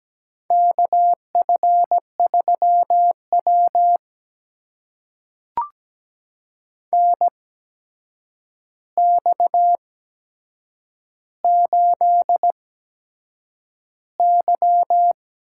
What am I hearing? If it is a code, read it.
Morse code, heard as a keyed tone, about 17 wpm, spaced as for 7 wpm, 700 Hz. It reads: KF3W NX8Y
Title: USA Call Signs - 4 Characters - 8x Character Spacing - 17wpm